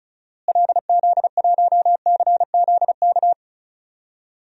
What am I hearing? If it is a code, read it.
Morse code, heard as a keyed tone, about 36 wpm, 700 Hz. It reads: LZ1CZK
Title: DXCC-100 Call Signs - 6 Characters - Repeated 3x - 35wpm